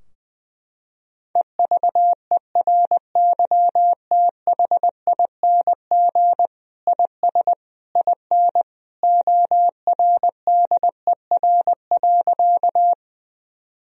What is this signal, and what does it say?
Morse code, heard as a keyed tone, about 20 wpm, 700 Hz. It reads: EVERYTHING IS IN ORDER.
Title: Sentences from Top 1000 Words - Review Part 2 - 20wpm